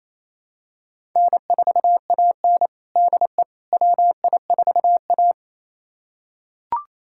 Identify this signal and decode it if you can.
Morse code, heard as a keyed tone, about 28 wpm, 700 Hz. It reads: N4AN DE WI4A
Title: USA Callsign DE Callsign - 4 Characters - 28wpm